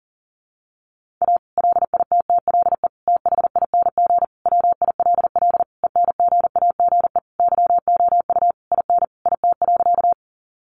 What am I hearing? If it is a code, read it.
Morse code, heard as a keyed tone, about 40 wpm, 700 Hz. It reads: A LITTLE THING WILL ENGAGE YOU IN IT.
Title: Sentences from Top 1500 Words 40wpm